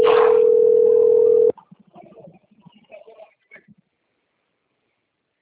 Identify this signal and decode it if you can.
Ringback tone (North America), 440 + 480 Hz.